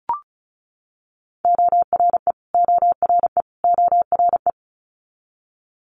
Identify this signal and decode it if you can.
Morse code, heard as a keyed tone, about 36 wpm, 700 Hz. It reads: ORE ORE ORE
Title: Multi-letter Phonemes Repeated 3x 35wpm